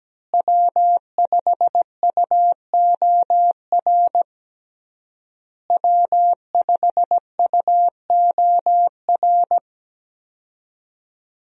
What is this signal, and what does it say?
Morse code, heard as a keyed tone, about 17 wpm, 700 Hz. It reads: W5UOR W5UOR
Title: USA Call Signs - 5 Characters - Repeated 3x - 17wpm